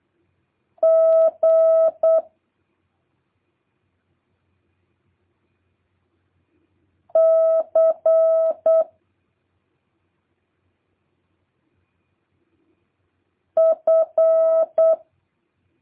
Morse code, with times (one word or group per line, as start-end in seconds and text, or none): 0.8-2.2 G
7.1-8.8 C
13.6-14.9 F